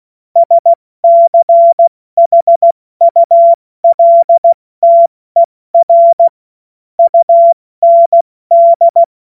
Morse code, heard as a keyed tone, about 16 wpm, 680 Hz. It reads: SCHULTER UND
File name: Morsekurs / Witze / Witz 16 Januar 2024 70 F80